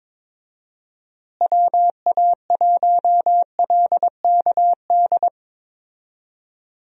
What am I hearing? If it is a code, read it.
Morse code, heard as a keyed tone, about 21 wpm, 700 Hz. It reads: WA1LKD
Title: USA Call Signs - 6 Characters - Repeated 3x - 22wpm